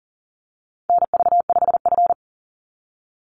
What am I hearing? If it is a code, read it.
Morse code, heard as a keyed tone, about 40 wpm, 700 Hz. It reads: NV5F
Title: Call Signs - 40wpm